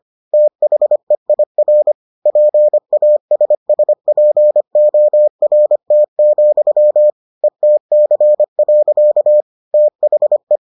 Morse code, heard as a keyed tone, about 25 wpm, 600 Hz. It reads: THEIR PASSPORT, ETC. THE